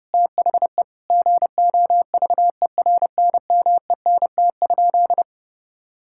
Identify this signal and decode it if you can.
Morse code, heard as a keyed tone, about 30 wpm, 700 Hz. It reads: THE GOVERNMENT?
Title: Sentences from Top 1600 Words 30wpm